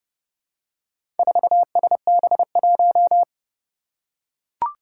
Morse code, heard as a keyed tone, about 30 wpm, 700 Hz. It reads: 4SB1
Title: Four Challenging Letter-Number Combinations 30wpm